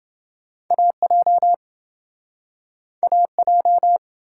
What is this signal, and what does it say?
Morse code, heard as a keyed tone, about 28 wpm, 700 Hz. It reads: AJ AJ